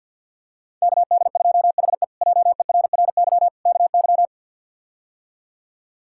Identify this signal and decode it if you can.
Morse code, heard as a keyed tone, about 49 wpm, 700 Hz. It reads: KD2HE JERRY KY
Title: K1USN SST QSO Practice - 50wpm